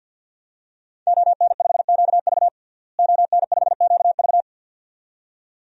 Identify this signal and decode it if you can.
Morse code, heard as a keyed tone, about 51 wpm, 700 Hz. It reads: ON5QV ON5QV
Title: DXCC-100 Call Signs - 5 Characters - Repeated 2x - 50wpm